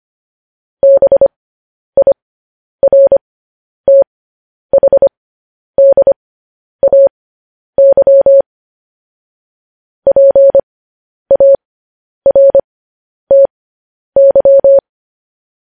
Morse code, heard as a keyed tone, about 25 wpm, spaced as for 10 wpm, 550 Hz. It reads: BIRTHDAY PARTY
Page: news-headlines-in-morse-code-at-10-wpm